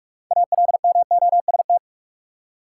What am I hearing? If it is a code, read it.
Morse code, heard as a keyed tone, about 46 wpm, 700 Hz. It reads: ALMOST
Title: Six Letter Words - 45wpm